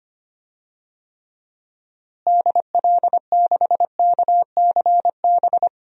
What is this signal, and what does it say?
Morse code, heard as a keyed tone, about 25 wpm, 700 Hz. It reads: DL6KCB
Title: DXCC-100 Call Signs - 6 Characters - Repeated 3x - 25wpm